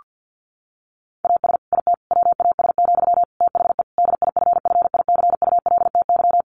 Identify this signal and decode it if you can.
Morse code, heard as a keyed tone, about 50 wpm, 700 Hz. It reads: AS IT WAS, THE DIFFICULTY
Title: Sentences from Top 1500 Words 50wpm